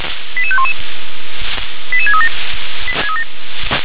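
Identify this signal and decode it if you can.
Five-tone selective call (ZVEI), heard as quick runs of five tones, tones of about 70 ms. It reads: 88511 88637
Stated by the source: ZVEI five-tone call received on an SDR.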